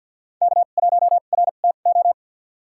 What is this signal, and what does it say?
Morse code, heard as a keyed tone, about 52 wpm, spaced as for 36 wpm, 700 Hz. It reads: K1RTO
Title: USA Call Signs - 5 Characters - 2x Character Spacing - 50wpm